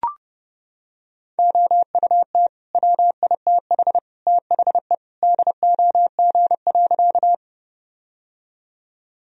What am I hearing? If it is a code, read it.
Morse code, heard as a keyed tone, about 30 wpm, 700 Hz. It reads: OUT WITH THE DOG.
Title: Sentences from Top 1000 Words - Review Part 2 - 30wpm